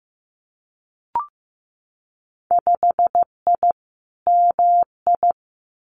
Morse code, heard as a keyed tone, about 15 wpm, 700 Hz. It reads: 5I MI
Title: Winter Field Day Exchanges Encore 15wpm